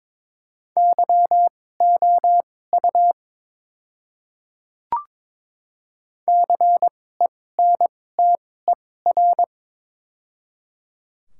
Morse code, heard as a keyed tone, about 22 wpm, spaced as for 16 wpm, 700 Hz. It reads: YOU CENTER
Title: 2x Character Spacing - Top 500 Words - 22wpm